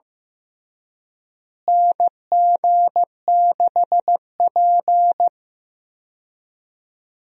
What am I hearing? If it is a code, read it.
Morse code, heard as a keyed tone, about 15 wpm, 700 Hz. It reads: NG6P
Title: USA Call Signs - 4 Characters - Repeated 3x - 15wpm